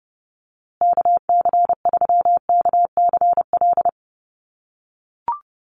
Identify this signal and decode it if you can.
Morse code, heard as a keyed tone, about 30 wpm, 700 Hz. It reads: KC3KCL
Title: USA Call Signs Repeated 3x 30wpm